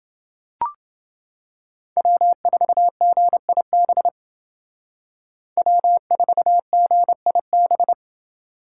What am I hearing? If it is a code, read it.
Morse code, heard as a keyed tone, about 29 wpm, 700 Hz. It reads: W4GIB W4GIB